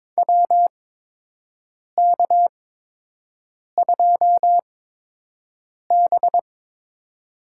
Morse code, heard as a keyed tone, about 22 wpm, spaced as for 6 wpm, 700 Hz. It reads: WK2B
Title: USA Call Signs - 8x Character Spacing - 22wpm